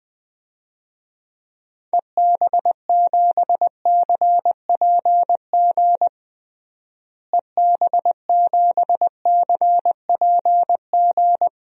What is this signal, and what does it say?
Morse code, heard as a keyed tone, about 20 wpm, 700 Hz. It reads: EB7CPG EB7CPG
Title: DXCC-100 Call Signs - 6 Characters - Repeated 2x - 20wpm